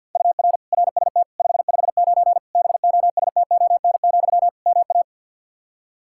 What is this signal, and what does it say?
Morse code, heard as a keyed tone, about 48 wpm, 700 Hz. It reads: UR RST 559 BOSTON, MA
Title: K1USN SST QSO Practice 50wpm